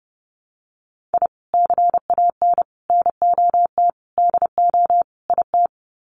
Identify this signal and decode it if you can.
Morse code, heard as a keyed tone, about 30 wpm, 700 Hz. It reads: I CAN NOT DO IT
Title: Sets of 5 Words Encore 30wpm